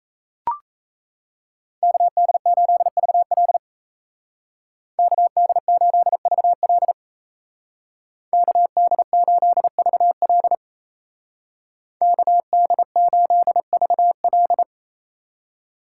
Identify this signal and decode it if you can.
Morse code, heard as a keyed tone, about 34 wpm, 700 Hz. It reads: KD8VL KD8VL KD8VL KD8VL
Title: Speed Racing - USA 2x2 Call Signs 28wpm